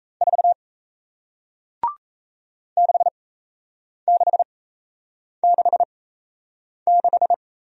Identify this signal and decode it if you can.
Morse code, heard as a keyed tone, about 38 wpm, spaced as for 8 wpm, 700 Hz. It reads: 4 6666